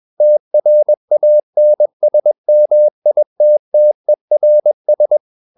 Morse code, heard as a keyed tone, about 21 wpm, 600 Hz. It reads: TRANSMITTERS
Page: CW med Gnister